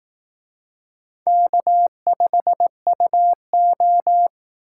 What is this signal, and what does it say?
Morse code, heard as a keyed tone, about 18 wpm, 700 Hz. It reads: K5UO
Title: Speed Racing - USA 1x2 Call Signs 15wpm